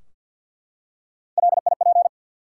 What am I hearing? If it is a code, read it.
Morse code, heard as a keyed tone, about 52 wpm, 700 Hz. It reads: LIP
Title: Three Letter Words - 50wpm